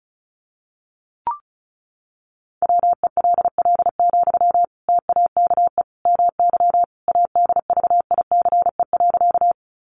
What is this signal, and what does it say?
Morse code, heard as a keyed tone, about 34 wpm, 700 Hz. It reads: WELL, TAKE MY ADVICE.
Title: Sentences from Top 1100 Words 35wpm